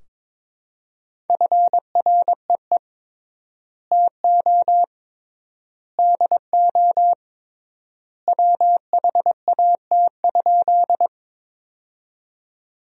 Morse code, heard as a keyed tone, about 22 wpm, 700 Hz. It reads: FREE TO DO WHAT?
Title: 3x Word-Spacing - Sentences from Top 500 Words - 22wpm